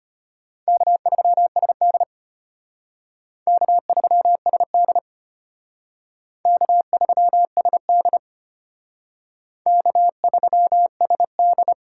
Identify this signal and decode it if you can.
Morse code, heard as a keyed tone, about 32 wpm, 700 Hz. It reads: K3SD K3SD K3SD K3SD